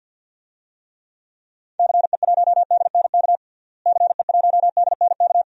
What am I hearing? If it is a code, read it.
Morse code, heard as a keyed tone, about 46 wpm, 700 Hz. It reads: CE1DNK CE1DNK
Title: DXCC-100 Call Signs - 6 Characters - Repeated 2x - 50wpm